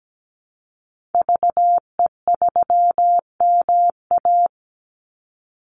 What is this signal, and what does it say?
Morse code, heard as a keyed tone, about 17 wpm, 700 Hz. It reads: VE3MA
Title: Call Signs Repeated 3x - 17wpm